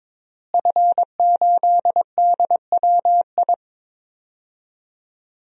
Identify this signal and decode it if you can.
Morse code, heard as a keyed tone, about 22 wpm, 700 Hz. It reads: F8DWI
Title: DXCC-100 Call Signs - 5 Characters - Repeated 3x - 22wpm